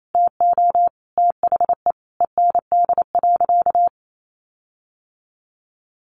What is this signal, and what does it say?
Morse code, heard as a keyed tone, about 28 wpm, 700 Hz. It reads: TO THE END.
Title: Sentences from Top 1300 Words 28wpm